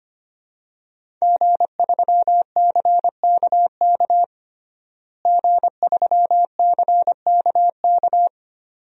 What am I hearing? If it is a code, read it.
Morse code, heard as a keyed tone, about 25 wpm, 700 Hz. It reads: G3CKK G3CKK
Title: DXCC-100 Call Signs - 5 Characters - Repeated 2x - 25wpm